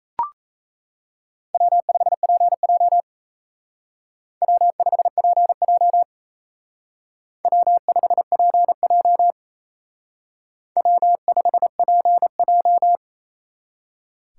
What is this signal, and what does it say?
Morse code, heard as a keyed tone, about 35 wpm, 700 Hz. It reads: W5PJ W5PJ W5PJ W5PJ
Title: Speed Racing - USA 1x2 Call Signs 28wpm